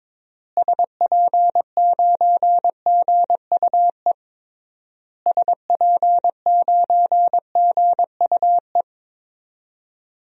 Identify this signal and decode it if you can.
Morse code, heard as a keyed tone, about 22 wpm, 700 Hz. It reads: SP9GUE SP9GUE
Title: DXCC-100 Call Signs - 6 Characters - Repeated 3x - 22wpm